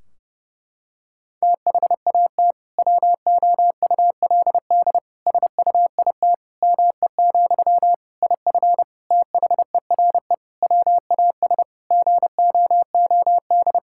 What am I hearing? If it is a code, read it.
Morse code, heard as a keyed tone, about 32 wpm, 700 Hz. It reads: THAT WOULD SUIT ME, IF THERE WAS GOOD
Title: Sentences from Top 1600 Words 30wpm